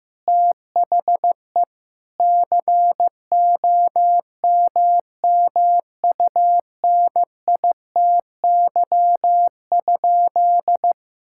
Morse code, heard as a keyed tone, about 15 wpm, 700 Hz. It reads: THE COMMUNITY?
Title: Sentences from Top 1000 Words - Review Part 5 - 15wpm